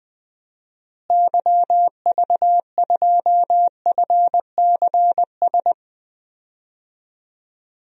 Morse code, heard as a keyed tone, about 21 wpm, 700 Hz. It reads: YV2FCS
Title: DXCC-100 Call Signs - 6 Characters - Repeated 2x - 20wpm